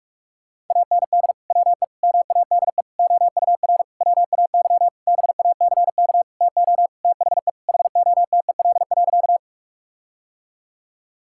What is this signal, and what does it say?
Morse code, heard as a keyed tone, about 46 wpm, 700 Hz. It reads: AND WE MADE OUR WAY BACK TO THE HOTEL.
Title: Sentences from Top 1600 Words 45wpm